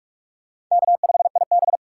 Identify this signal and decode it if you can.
Morse code, heard as a keyed tone, about 46 wpm, 700 Hz. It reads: K5IB
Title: USA Call Signs - 4 Characters - 2x Character Spacing - 45wpm